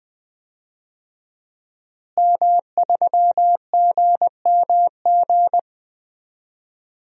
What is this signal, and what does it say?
Morse code, heard as a keyed tone, about 20 wpm, 700 Hz. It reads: M3GMG
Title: DXCC-100 Call Signs - 5 Characters - Repeated 3x - 20wpm